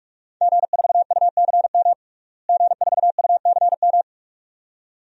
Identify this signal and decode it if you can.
Morse code, heard as a keyed tone, about 46 wpm, 700 Hz. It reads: G4UCM G4UCM
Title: DXCC-100 Call Signs - 5 Characters - Repeated 2x - 45wpm